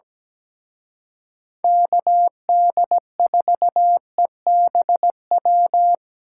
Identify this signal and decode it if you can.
Morse code, heard as a keyed tone, about 20 wpm, 700 Hz. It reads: KD4EBW
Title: Speed Racing - USA 2x3 Call Signs 17wpm